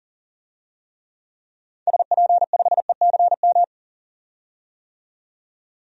Morse code, heard as a keyed tone, about 38 wpm, 700 Hz. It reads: SP5ECM
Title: DXCC-100 Call Signs - 6 Characters - 40wpm